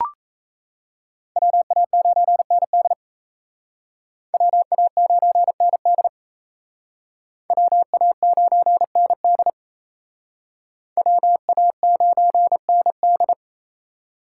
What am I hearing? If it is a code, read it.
Morse code, heard as a keyed tone, about 36 wpm, 700 Hz. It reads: WA9ND WA9ND WA9ND WA9ND